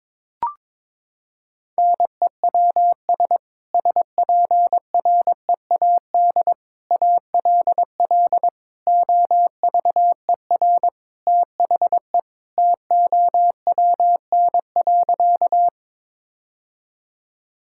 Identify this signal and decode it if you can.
Morse code, heard as a keyed tone, about 22 wpm, 700 Hz. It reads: NEWS SPREAD ALL OVER THE TOWN.
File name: practice / 22wpm / Sentences with 3-Word Phrases - 22wpm